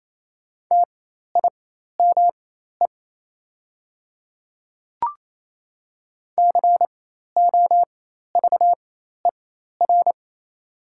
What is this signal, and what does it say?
Morse code, heard as a keyed tone, about 27 wpm, spaced as for 13 wpm, 700 Hz. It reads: TIME COVER
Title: 4x Character Spacing - Top 500 Words - 28wpm